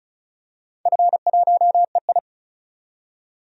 Morse code, heard as a keyed tone, about 36 wpm, 700 Hz. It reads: F1EI
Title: DXCC-100 Call Signs - 4 Characters - 35wpm